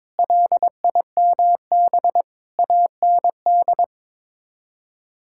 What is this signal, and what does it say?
Morse code, heard as a keyed tone, about 22 wpm, 700 Hz. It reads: LIMB AND
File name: practice / 22wpm / Sets of 8 Words - 22wpm